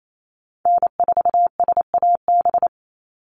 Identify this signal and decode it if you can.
Morse code, heard as a keyed tone, about 28 wpm, 700 Hz. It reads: N4SAB
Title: USA Call Signs 28wpm